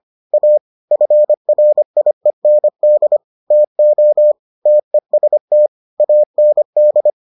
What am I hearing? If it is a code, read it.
Morse code, heard as a keyed tone, about 25 wpm, 600 Hz. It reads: A FRIEND TO TEST AND